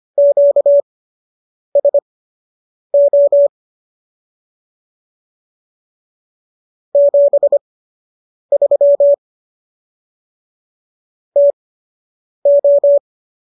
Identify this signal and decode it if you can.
Morse code, heard as a keyed tone, about 25 wpm, spaced as for 8 wpm, 570 Hz. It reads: QSO 73 TO